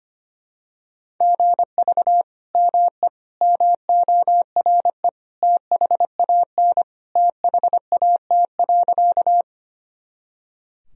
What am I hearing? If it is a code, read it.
Morse code, heard as a keyed tone, about 25 wpm, 700 Hz. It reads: GV ME MORE THAN THAT.
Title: Sentences from Top 200 Words and Abbreviations 25wpm